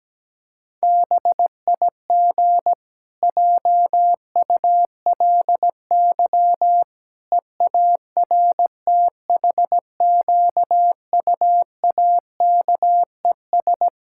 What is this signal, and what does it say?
Morse code, heard as a keyed tone, about 17 wpm, 700 Hz. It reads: BIG JULY EARTHQUAKES
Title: Pangram Challenge v2 - 17wpm